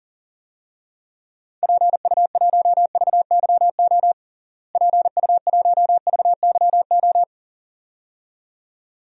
Morse code, heard as a keyed tone, about 40 wpm, 700 Hz. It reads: PU1VYO PU1VYO
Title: DXCC-100 Call Signs - 6 Characters - Repeated 2x - 40wpm